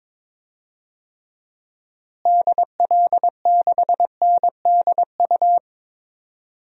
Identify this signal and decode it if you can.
Morse code, heard as a keyed tone, about 22 wpm, 700 Hz. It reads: DL6NDU
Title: DXCC-100 Call Signs - 6 Characters - Repeated 2x - 22wpm